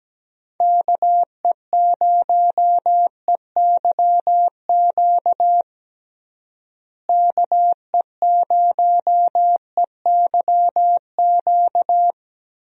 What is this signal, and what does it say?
Morse code, heard as a keyed tone, about 17 wpm, 700 Hz. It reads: KE0EYQ KE0EYQ
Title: USA Call Signs - 6 Characters - Repeated 2x - 17wpm